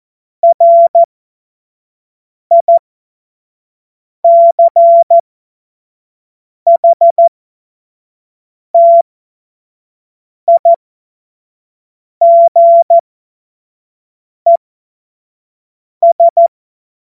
Morse code, heard as a keyed tone, about 14 wpm, spaced as for 5 wpm, 680 Hz. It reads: RICHTIGES
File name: Morsekurs / Witze / Witz 21. September 2021, 25 F70